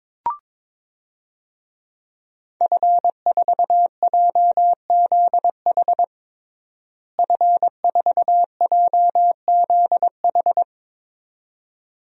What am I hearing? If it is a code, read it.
Morse code, heard as a keyed tone, about 22 wpm, 700 Hz. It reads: F4JZH F4JZH
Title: DXCC-100 Call Signs - 5 Characters - Repeated 2x - 22wpm